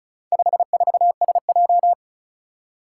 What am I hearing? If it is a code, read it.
Morse code, heard as a keyed tone, about 36 wpm, 700 Hz. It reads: 54SJ
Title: Four Challenging Letter-Number Combinations 35wpm